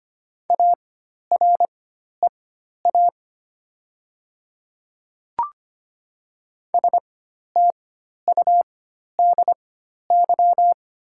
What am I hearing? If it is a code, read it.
Morse code, heard as a keyed tone, about 25 wpm, spaced as for 12 wpm, 700 Hz. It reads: AREA STUDY
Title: 4x Character Spacing - Top 500 Words - 25wpm